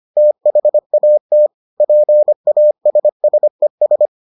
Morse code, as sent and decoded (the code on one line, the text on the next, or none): - .... .- - / .--. .- ... ... ...
THAT PASSES